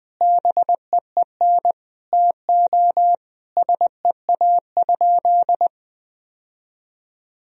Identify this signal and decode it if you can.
Morse code, heard as a keyed tone, about 21 wpm, 700 Hz. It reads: BEEN TO SEA?
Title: Sentences from Top 1000 Words - Review Part 1 - 20wpm